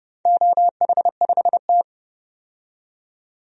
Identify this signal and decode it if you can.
Morse code, heard as a keyed tone, about 32 wpm, 700 Hz. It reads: OH5T
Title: DXCC-100 Call Signs - 4 Characters - 30wpm